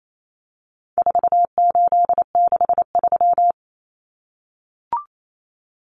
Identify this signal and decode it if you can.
Morse code, heard as a keyed tone, about 28 wpm, 700 Hz. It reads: 4863
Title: Common Cut Numbers - 4-Digit 28wpm